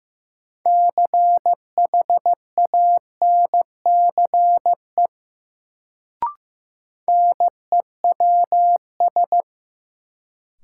Morse code, heard as a keyed tone, about 15 wpm, 700 Hz. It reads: CHANCE NEWS
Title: Sentences from Top 800 Words - New Word Drill - 15wpm